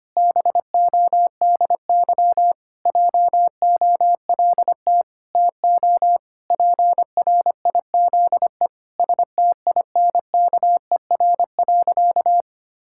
Morse code, heard as a keyed tone, about 25 wpm, 700 Hz. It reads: BODY JOLT TO PRIZE STINKER.
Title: Pangram Challenge v2 - 25wpm